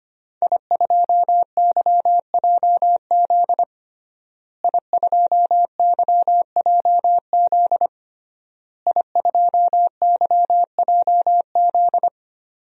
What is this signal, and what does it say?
Morse code, heard as a keyed tone, about 25 wpm, 700 Hz. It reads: I2YJZ I2YJZ I2YJZ
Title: DXCC-100 Call Signs - 5 Characters - Repeated 3x - 25wpm